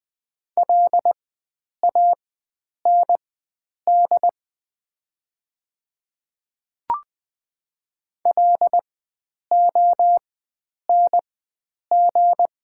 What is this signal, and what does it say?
Morse code, heard as a keyed tone, about 20 wpm, spaced as for 9 wpm, 700 Hz. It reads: LAND LONG
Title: 4x Character Spacing - Top 500 Words - 20wpm